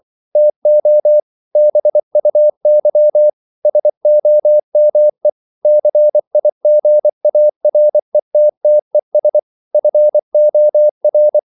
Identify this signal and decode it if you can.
Morse code, heard as a keyed tone, about 24 wpm, 600 Hz. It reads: TO BUY SOME CIGARETTES FOR